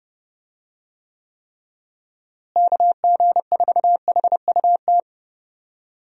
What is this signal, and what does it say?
Morse code, heard as a keyed tone, about 31 wpm, 700 Hz. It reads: KG4HUT
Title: USA Call Signs - 6 Characters - Repeated 3x - 30wpm